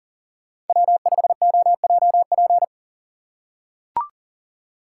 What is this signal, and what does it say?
Morse code, heard as a keyed tone, about 40 wpm, 700 Hz. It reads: W5OJP
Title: USA Call Signs - 5 Characters - 2x Character Spacing - 40wpm